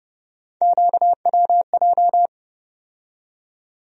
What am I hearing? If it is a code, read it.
Morse code, heard as a keyed tone, about 30 wpm, 700 Hz. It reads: QWJ
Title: USA Call Sign Postfixes - 3 Letters 30wpm